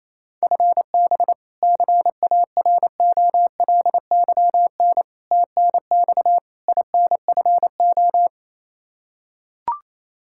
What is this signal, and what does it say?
Morse code, heard as a keyed tone, about 28 wpm, 700 Hz. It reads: FB CAROLYN TNX INFO
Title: Practice QSO - Exchange 3 - More Info - 28wpm